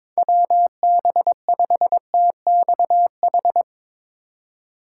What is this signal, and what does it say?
Morse code, heard as a keyed tone, about 22 wpm, 700 Hz. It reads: WB5TXH
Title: USA Call Signs - 6 Characters - 4x Character Spacing - 22wpm